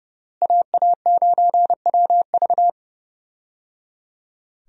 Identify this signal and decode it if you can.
Morse code, heard as a keyed tone, about 30 wpm, 700 Hz. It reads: AA9WV
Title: USA Call Signs - 5 Characters - Repeated 3x - 30wpm